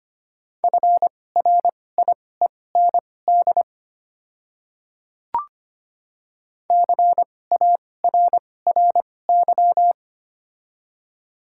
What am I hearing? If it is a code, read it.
Morse code, heard as a keyed tone, about 25 wpm, spaced as for 18 wpm, 700 Hz. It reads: FRIEND CARRY
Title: 2x Character Spacing - Top 500 Words - 25wpm